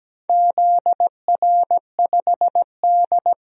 Morse code, heard as a keyed tone, about 17 wpm, 700 Hz. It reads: ZR5D
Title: DXCC-100 Call Signs - 4 Characters - 17wpm